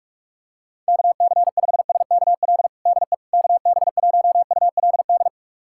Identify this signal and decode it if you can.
Morse code, heard as a keyed tone, about 46 wpm, 700 Hz. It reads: KC5SKL DE KB1ULD
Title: USA Callsign DE Callsign - 6 Characters - 45wpm